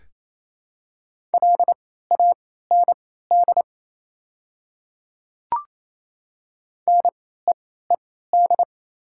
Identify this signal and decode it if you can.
Morse code, heard as a keyed tone, about 28 wpm, spaced as for 16 wpm, 700 Hz. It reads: LAND NEED
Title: 3x Character Spacing - Top 200 Words - 28wpm